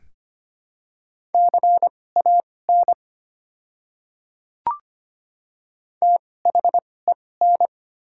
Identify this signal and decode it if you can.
Morse code, heard as a keyed tone, about 25 wpm, spaced as for 18 wpm, 700 Hz. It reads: CAN THEN
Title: 2x Character Spacing - Top 100 Words - 25wpm